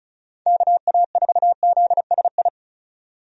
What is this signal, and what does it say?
Morse code, heard as a keyed tone, about 36 wpm, 700 Hz. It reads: KA4ZSI
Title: USA Call Signs - 6 Characters - 35wpm